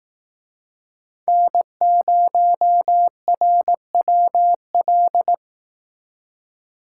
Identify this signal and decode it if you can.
Morse code, heard as a keyed tone, about 18 wpm, 700 Hz. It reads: N0RWL
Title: Speed Racing - USA 1x3 Call Signs 15wpm